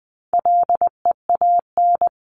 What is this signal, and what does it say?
Morse code, heard as a keyed tone, about 20 wpm, 700 Hz. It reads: LEAN
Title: 260 Lesson 42e - Words 20-20wpm